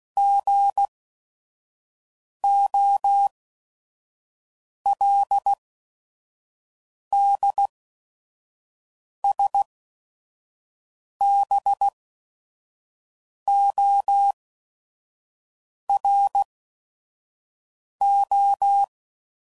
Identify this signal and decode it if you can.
Morse code, heard as a keyed tone, about 16 wpm, spaced as for 5 wpm, 800 Hz. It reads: GOLDSBORO
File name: CW Code Groups @5WPM / fromSM's / Cities